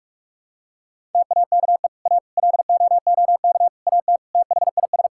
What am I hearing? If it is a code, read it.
Morse code, heard as a keyed tone, about 46 wpm, 700 Hz. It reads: TAKE A LOOK AT THIS
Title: Sentences from Top 1000 Words - Review Part 5 - 45wpm